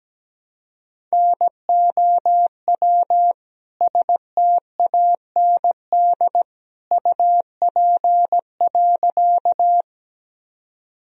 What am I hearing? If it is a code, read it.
Morse code, heard as a keyed tone, about 17 wpm, 700 Hz. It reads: NOW STAND UP.
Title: Sentences from Top 1000 Words - Review Part 1 - 17wpm